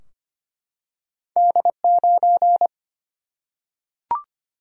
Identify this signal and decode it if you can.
Morse code, heard as a keyed tone, about 25 wpm, 700 Hz. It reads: D 9
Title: Single-Letter Single-Number 25wpm